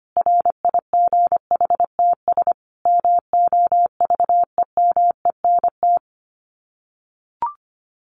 Morse code, heard as a keyed tone, about 25 wpm, 700 Hz. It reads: RIGHTS MOVEMENT
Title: Sets of 5 Words Encore2 25wpm